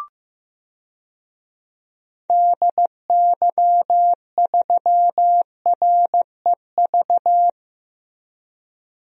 Morse code, heard as a keyed tone, about 15 wpm, 700 Hz. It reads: DY3REV
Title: DXCC-100 Call Signs - 6 Characters - Repeated 3x - 15wpm